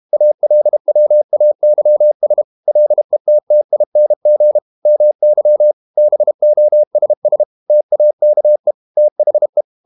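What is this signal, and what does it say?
Morse code, heard as a keyed tone, about 32 wpm, 600 Hz. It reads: ALWAYS LETTING MY BOSS TAKE THE
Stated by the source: Morse code practice at 32 wpm from CW med Gnister, reading Ken Follett's novel Never in English.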